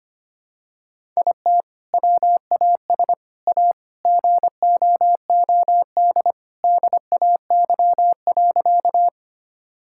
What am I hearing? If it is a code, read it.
Morse code, heard as a keyed tone, about 25 wpm, 700 Hz. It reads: IT WAS A GOOD DAY.
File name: practice / 25wpm / Sentences with 2-Word Phrases - 25wpm